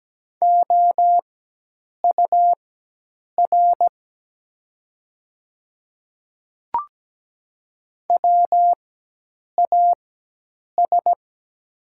Morse code, heard as a keyed tone, about 17 wpm, spaced as for 8 wpm, 700 Hz. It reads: OUR WAS
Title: 4x Character Spacing - Top 200 Words - 17wpm